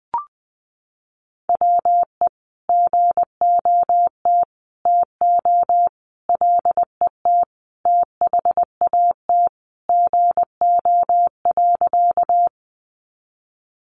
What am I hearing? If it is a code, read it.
Morse code, heard as a keyed tone, about 20 wpm, 700 Hz. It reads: WE GOT TO LET THAT GO.
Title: Sentences from Top 300 Words 20wpm